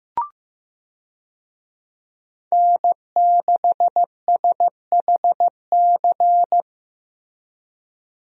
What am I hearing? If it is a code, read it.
Morse code, heard as a keyed tone, about 16 wpm, 700 Hz. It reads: N6SHC
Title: USA Call Signs - 5 Characters - Repeated 3x - 15wpm